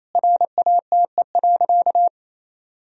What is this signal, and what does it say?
Morse code, heard as a keyed tone, about 28 wpm, 700 Hz. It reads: RATE.
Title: Sentences from Top 1000 Words - Review Part 4 - 28wpm